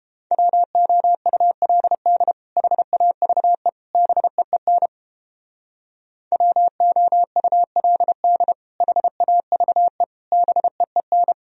Morse code, heard as a keyed tone, about 29 wpm, 700 Hz. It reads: WOULD HAVE BEEN WOULD HAVE BEEN